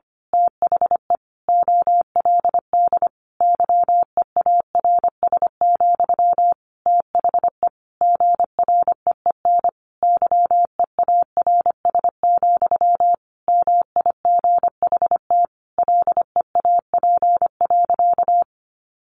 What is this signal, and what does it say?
Morse code, heard as a keyed tone, about 25 wpm, 700 Hz. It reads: THE OLD YEARS, THE GREEN YEARS, MIGHT LEAP.
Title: A Sound Of Thunder 25wpm